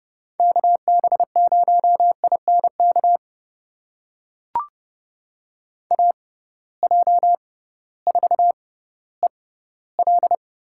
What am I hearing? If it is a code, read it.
Morse code, heard as a keyed tone, about 30 wpm, spaced as for 16 wpm, 700 Hz. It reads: KB0INK AJ4EL
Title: USA Call Signs - 6x Character Spacing - 30wpm